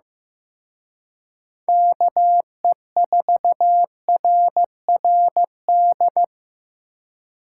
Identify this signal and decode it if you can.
Morse code, heard as a keyed tone, about 18 wpm, 700 Hz. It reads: KE4RRD
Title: Speed Racing - USA 2x3 Call Signs 15wpm